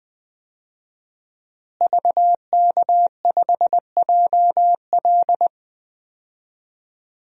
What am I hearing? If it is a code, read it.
Morse code, heard as a keyed tone, about 20 wpm, 700 Hz. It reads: VK5JL
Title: DXCC-100 Call Signs - 5 Characters - Repeated 3x - 20wpm